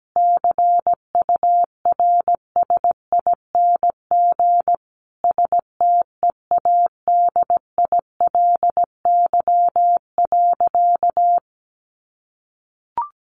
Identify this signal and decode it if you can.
Morse code, heard as a keyed tone, about 17 wpm, 700 Hz. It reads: CURSING STEADILY.
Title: A Sound Of Thunder 17wpm